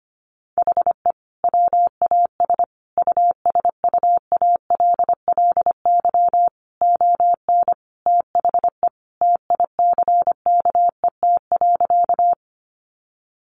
Morse code, heard as a keyed tone, about 25 wpm, 700 Hz. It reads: HE WAS USUALLY ON THE TICKET.
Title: Sentences from Top 1400 Words 25wpm